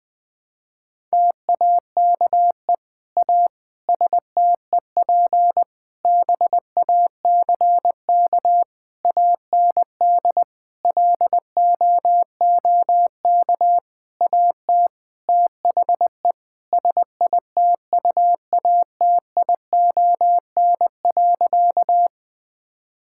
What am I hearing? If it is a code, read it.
Morse code, heard as a keyed tone, about 21 wpm, 700 Hz. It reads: TAKE A STEP BACK AND LOOK AT THE SITUATION.
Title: Sentences with 8-Word Phrases - Spoken Beginning - 20wpm